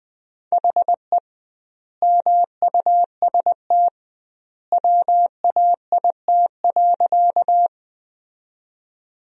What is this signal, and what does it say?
Morse code, heard as a keyed tone, about 20 wpm, 700 Hz. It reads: HE MUST WAIT.
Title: 2x Word-Spacing - Sentences from Top 500 Words - 20wpm